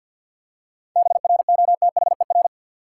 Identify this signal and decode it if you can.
Morse code, heard as a keyed tone, about 49 wpm, 700 Hz. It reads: BROTHER
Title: Seven Letter Words - 50wpm